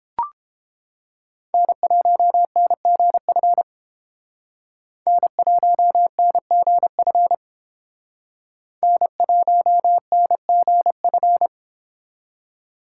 Morse code, heard as a keyed tone, about 28 wpm, 700 Hz. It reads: N1NGF N1NGF N1NGF